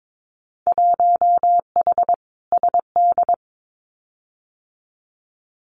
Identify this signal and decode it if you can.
Morse code, heard as a keyed tone, about 22 wpm, 700 Hz. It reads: 1H SD
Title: Winter Field Day Exchanges 22wpm